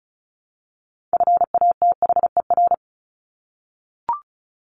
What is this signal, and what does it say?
Morse code, heard as a keyed tone, about 34 wpm, 700 Hz. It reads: FATHER
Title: Top 200 Words Repeated 35wpm